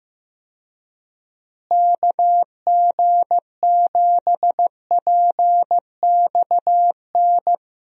Morse code, heard as a keyed tone, about 15 wpm, 700 Hz. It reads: KG7PXN
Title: USA Call Signs - 6 Characters - Repeated 2x - 15wpm